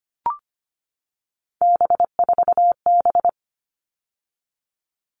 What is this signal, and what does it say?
Morse code, heard as a keyed tone, about 25 wpm, 700 Hz. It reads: B4B
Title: All Cut Numbers - 3-Digit 25wpm